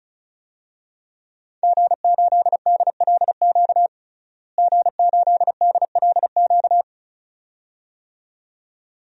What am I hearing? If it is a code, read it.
Morse code, heard as a keyed tone, about 34 wpm, 700 Hz. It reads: G8DLQ G8DLQ
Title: DXCC-100 Call Signs - 5 Characters - Repeated 2x - 35wpm